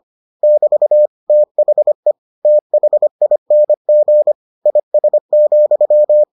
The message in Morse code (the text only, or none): <BT> THE THING IS,